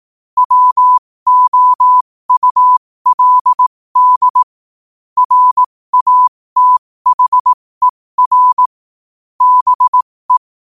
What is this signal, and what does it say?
Morse code, heard as a keyed tone, about 18 wpm, spaced as for 15 wpm, 1000 Hz. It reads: WOULD RATHER BE
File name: qotdfiles / 15wpm